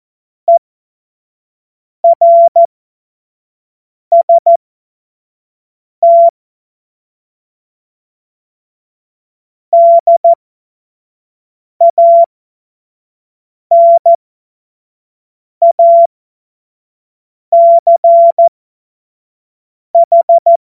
Morse code, heard as a keyed tone, about 14 wpm, spaced as for 5 wpm, 680 Hz. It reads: ERST DANACH
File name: Morsekurs / Witze / Witz 07. Dezember 2021, 25 F70